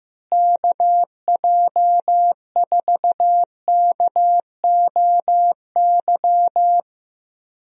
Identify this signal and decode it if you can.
Morse code, heard as a keyed tone, about 15 wpm, 700 Hz. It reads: KJ4KOY
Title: USA Call Signs - 6 Characters - Repeated 3x - 15wpm